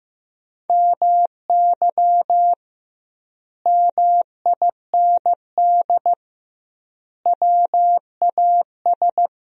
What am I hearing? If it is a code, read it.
Morse code, heard as a keyed tone, about 15 wpm, 700 Hz. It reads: MY MIND WAS
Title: 2x Word-Spacing - Sentences from Top 500 Words - 15wpm